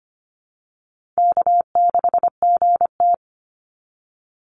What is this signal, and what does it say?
Morse code, heard as a keyed tone, about 25 wpm, 700 Hz. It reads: K6GT
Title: Easy Call Signs Repeated 2x - 25wpm